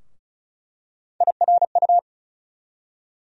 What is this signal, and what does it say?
Morse code, heard as a keyed tone, about 36 wpm, 700 Hz. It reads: IRU
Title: USA Call Sign Postfixes - 3 Letters 35wpm